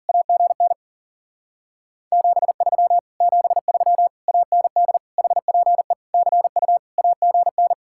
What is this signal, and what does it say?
Morse code, heard as a keyed tone, about 50 wpm, 700 Hz. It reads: AGN 73 73 AND HPE CU AGN